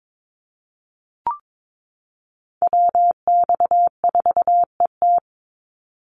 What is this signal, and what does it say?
Morse code, heard as a keyed tone, about 22 wpm, 700 Hz. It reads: WX4ET
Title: NA SOTA Chaser Callsigns 22wpm